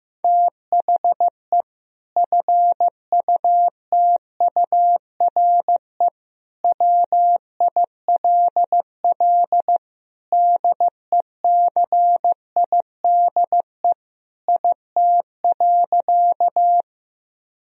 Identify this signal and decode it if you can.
Morse code, heard as a keyed tone, about 15 wpm, 700 Hz. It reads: THE FUTURE WILL DECIDE IT.
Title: Sentences from Top 1000 Words - Review Part 3 - 15wpm